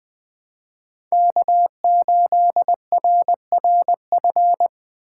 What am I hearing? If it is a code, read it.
Morse code, heard as a keyed tone, about 20 wpm, 700 Hz. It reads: K8RRF
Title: USA Call Signs - 5 Characters - 4x Character Spacing - 20wpm